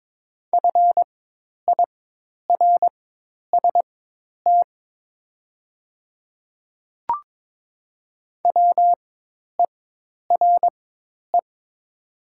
Morse code, heard as a keyed tone, about 22 wpm, spaced as for 10 wpm, 700 Hz. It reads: FIRST WERE
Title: 4x Character Spacing - Top 500 Words - 22wpm